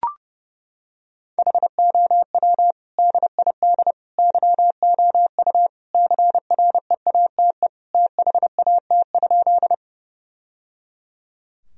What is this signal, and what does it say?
Morse code, heard as a keyed tone, about 30 wpm, 700 Hz. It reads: HOW DID YOU CREATE THAT?
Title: Sentences from Top 1000 Words - Review Part 3 - 30wpm